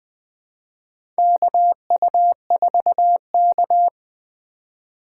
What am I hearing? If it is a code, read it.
Morse code, heard as a keyed tone, about 20 wpm, 700 Hz. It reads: KU4K
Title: USA Call Signs - 4 Characters - 3x Character Spacing - 20wpm